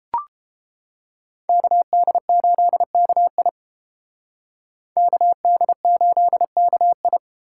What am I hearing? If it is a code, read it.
Morse code, heard as a keyed tone, about 28 wpm, 700 Hz. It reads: KD8KI KD8KI